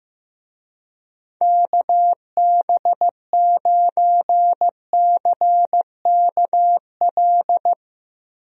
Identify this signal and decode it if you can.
Morse code, heard as a keyed tone, about 15 wpm, 700 Hz. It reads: KB9CKL
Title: USA Call Signs - 6 Characters - Repeated 2x - 15wpm